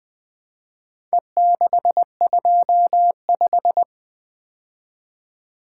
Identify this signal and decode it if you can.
Morse code, heard as a keyed tone, about 24 wpm, 700 Hz. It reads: E625
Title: Speed-Racing - Four Letter-Number Combinations - 20wpm